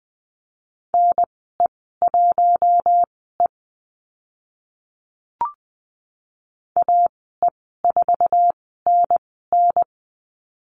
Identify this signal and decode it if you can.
Morse code, heard as a keyed tone, about 20 wpm, spaced as for 14 wpm, 700 Hz. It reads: NE1E AE4NN
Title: 048 Lesson 9g - Number 4 - Call Signs 20wpm